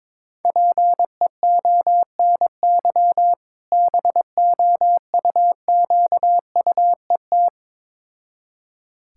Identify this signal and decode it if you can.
Morse code, heard as a keyed tone, about 22 wpm, 690 Hz. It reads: PEONY BOUQUET
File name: Difficult Words with Hints - 22wpm